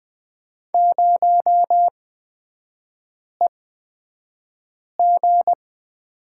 Morse code, heard as a keyed tone, about 20 wpm, spaced as for 5 wpm, 700 Hz. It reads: 0EG
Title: 243 Lesson 39e - Letter z - Three Characters 20wpm